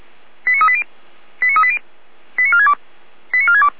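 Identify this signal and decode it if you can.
Five-tone selective call (ZVEI), heard as quick runs of five tones, tones of about 70 ms. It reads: 89380 89380 89462 89462
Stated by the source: ZVEI five-tone selective call 89465 received on SDR.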